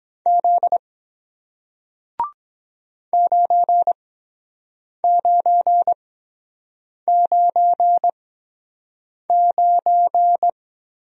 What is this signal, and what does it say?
Morse code, heard as a keyed tone, about 23 wpm, spaced as for 7 wpm, 700 Hz. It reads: Z 9999